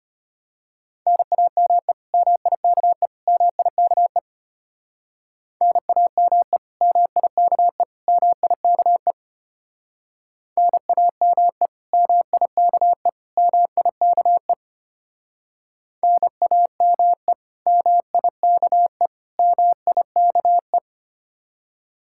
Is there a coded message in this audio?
Yes: Morse code from 1.1 to 20.8 s, heard as a keyed tone, about 32 wpm, 700 Hz. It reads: NAME MIKE MIKE NAME MIKE MIKE NAME MIKE MIKE NAME MIKE MIKE